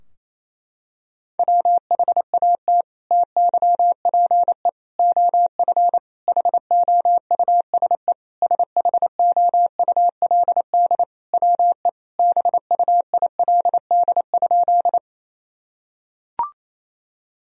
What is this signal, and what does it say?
Morse code, heard as a keyed tone, about 28 wpm, 700 Hz. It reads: WHAT TYPE OF HOUSE SHOULD WE BUILD?
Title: Sentences from Top 1000 Words - Review Part 3 - 28wpm